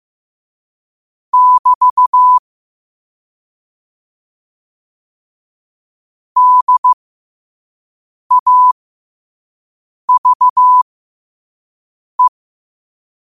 Morse code, heard as a keyed tone, about 15 wpm, spaced as for 5 wpm, 1000 Hz. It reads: <BT> DAVE
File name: qotdfiles / 5wpm